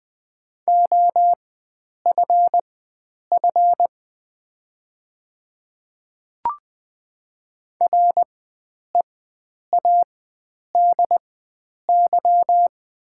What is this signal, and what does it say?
Morse code, heard as a keyed tone, about 20 wpm, spaced as for 9 wpm, 700 Hz. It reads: OFF READY
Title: 4x Character Spacing - Top 500 Words - 20wpm